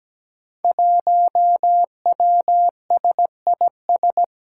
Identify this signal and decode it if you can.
Morse code, heard as a keyed tone, about 17 wpm, 700 Hz. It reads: 1WSIS
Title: Five Challenging Letter-Number Combinations 17wpm